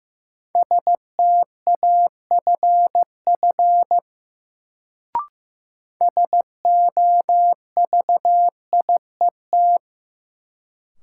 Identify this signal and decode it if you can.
Morse code, heard as a keyed tone, about 15 wpm, 700 Hz. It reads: STAFF SOVIET
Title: Sets of 5 Words Encore - New Word Drill - 15wpm